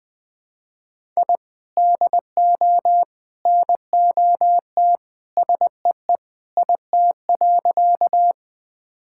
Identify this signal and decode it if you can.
Morse code, heard as a keyed tone, about 20 wpm, 700 Hz. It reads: I DO NOT SEE IT.
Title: Sentences with 3-Word Phrases - 20wpm